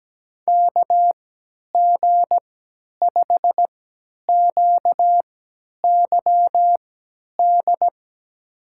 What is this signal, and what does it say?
Morse code, heard as a keyed tone, about 17 wpm, spaced as for 10 wpm, 700 Hz. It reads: KG5QYD
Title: USA Call Signs - 6 Characters - 3x Character Spacing - 17wpm